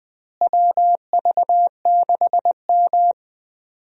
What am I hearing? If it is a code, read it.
Morse code, heard as a keyed tone, about 21 wpm, 700 Hz. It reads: WV6M
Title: USA Call Signs - 4 Characters - Repeated 3x - 20wpm